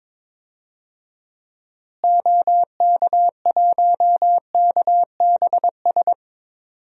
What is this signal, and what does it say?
Morse code, heard as a keyed tone, about 22 wpm, 700 Hz. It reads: OK1KBS
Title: DXCC-100 Call Signs - 6 Characters - Repeated 2x - 22wpm